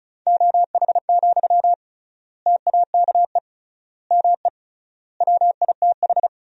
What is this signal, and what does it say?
Morse code, heard as a keyed tone, about 34 wpm, 700 Hz. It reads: OH, TAKE ME WITH
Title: 3x Word-Spacing - Sentences from Top 500 Words - 35wpm